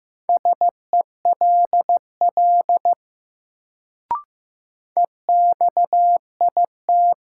Morse code, heard as a keyed tone, about 15 wpm, 700 Hz. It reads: SELL EXIT
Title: Sentences from Top 1100 Words - New Word Drill Alt - 15wpm